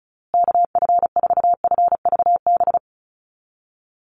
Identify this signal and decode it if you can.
Morse code, heard as a keyed tone, about 34 wpm, 700 Hz. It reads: KF4FVB
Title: USA Call Signs Repeated 3x 35wpm